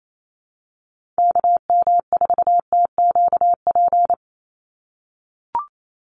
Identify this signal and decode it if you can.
Morse code, heard as a keyed tone, about 28 wpm, 700 Hz. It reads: KM4TQP
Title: USA Call Signs 28wpm